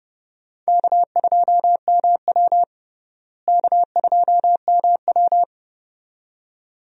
Morse code, heard as a keyed tone, about 30 wpm, 700 Hz. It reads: K2MW K2MW
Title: USA Call Signs - 4 Characters - Repeated 2x - 30wpm